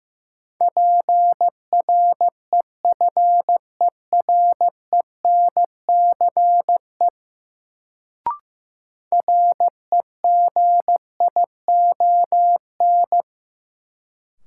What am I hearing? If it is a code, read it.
Morse code, heard as a keyed tone, about 15 wpm, 700 Hz. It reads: PREFERENCE REGION
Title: Sentences from Top 1500 Words - New Word Drill - 15wpm